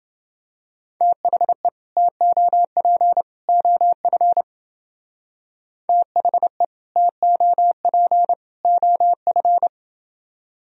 Morse code, heard as a keyed tone, about 29 wpm, 700 Hz. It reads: THE TOP OF THE TOP OF